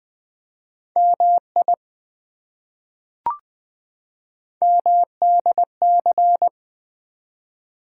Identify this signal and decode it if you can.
Morse code, heard as a keyed tone, about 20 wpm, 700 Hz. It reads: MI MDC
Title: ARRL Sections v2 - 20wpm